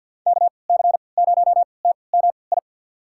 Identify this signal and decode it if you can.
Morse code, heard as a keyed tone, about 47 wpm, spaced as for 28 wpm, 700 Hz. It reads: KC0TMI
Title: USA Call Signs - 6 Characters - 3x Character Spacing - 50wpm